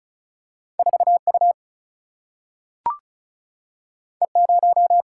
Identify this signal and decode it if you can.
Morse code, heard as a keyed tone, about 36 wpm, 700 Hz. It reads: 4U E0
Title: Single-Letter Single-Number 35wpm